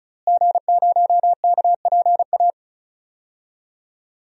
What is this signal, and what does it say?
Morse code, heard as a keyed tone, about 36 wpm, 700 Hz. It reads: G0KPA
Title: DXCC-100 Call Signs - 5 Characters - Repeated 2x - 35wpm